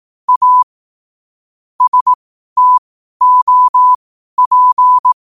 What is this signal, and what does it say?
Morse code, heard as a keyed tone, about 18 wpm, spaced as for 12 wpm, 1000 Hz. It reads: A STOP
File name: qotdfiles / 12wpm